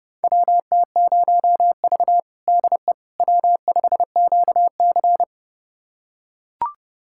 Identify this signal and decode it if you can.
Morse code, heard as a keyed tone, about 32 wpm, 700 Hz. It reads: WT0V DE W5QC
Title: USA Callsign DE Callsign - 4 Characters - 30wpm